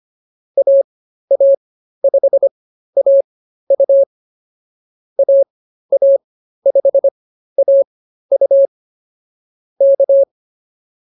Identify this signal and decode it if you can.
Morse code, heard as a keyed tone, about 24 wpm, spaced as for 13 wpm, 560 Hz. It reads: AA5AU AA5AU K
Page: intermediate-practice-files